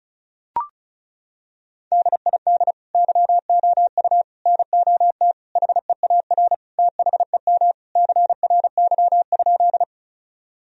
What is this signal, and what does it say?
Morse code, heard as a keyed tone, about 36 wpm, 700 Hz. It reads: DID YOU NOT HEAR THEM CRY?
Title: Sentences from Top 1000 Words - Review Part 2 - 35wpm